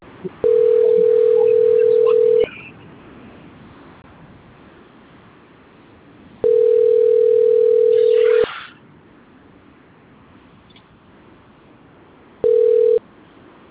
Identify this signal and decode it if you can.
Ringback tone (North America), 440 + 480 Hz.